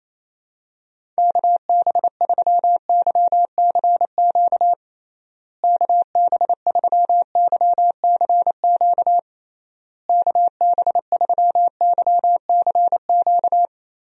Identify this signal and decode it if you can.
Morse code, heard as a keyed tone, about 28 wpm, 700 Hz. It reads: KB3YCQ KB3YCQ KB3YCQ
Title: USA Call Signs - 6 Characters - Repeated 3x - 28wpm